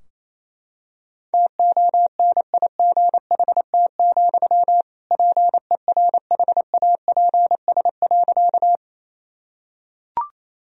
Morse code, heard as a keyed tone, about 28 wpm, 700 Hz. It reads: TONIGHT, PERHAPS.
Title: Sentences from Top 1000 Words - Review Part 3 - 28wpm